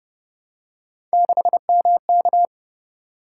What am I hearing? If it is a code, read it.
Morse code, heard as a keyed tone, about 30 wpm, 700 Hz. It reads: 6MK